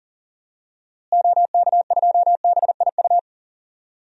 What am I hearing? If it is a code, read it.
Morse code, heard as a keyed tone, about 37 wpm, 700 Hz. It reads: OK2BIU
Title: DXCC-100 Call Signs - 6 Characters - Repeated 2x - 40wpm